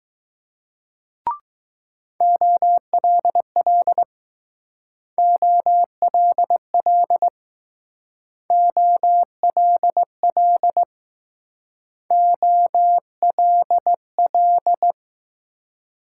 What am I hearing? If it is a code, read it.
Morse code, heard as a keyed tone, about 19 wpm, 700 Hz. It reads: OLL OLL OLL OLL